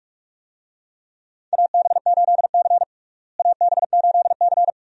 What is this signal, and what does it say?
Morse code, heard as a keyed tone, about 47 wpm, 700 Hz. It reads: AB8C AB8C
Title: USA Call Signs - 4 Characters - Repeated 2x - 45wpm